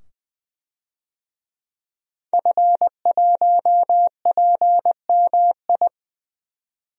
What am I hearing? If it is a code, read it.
Morse code, heard as a keyed tone, about 21 wpm, 700 Hz. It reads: F1PMI
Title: DXCC-100 Call Signs - 5 Characters - 20wpm